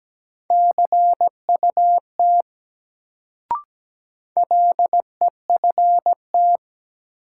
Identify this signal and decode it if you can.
Morse code, heard as a keyed tone, about 17 wpm, 700 Hz. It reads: CUT LEFT
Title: Sentences from Top 300 Words - New Word Drill - 17wpm